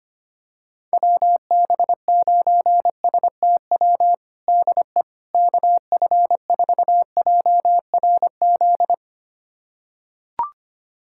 Morse code, heard as a keyed tone, about 25 wpm, 700 Hz. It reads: WB9STW DE KF4JRZ
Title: USA Callsign DE Callsign 25wpm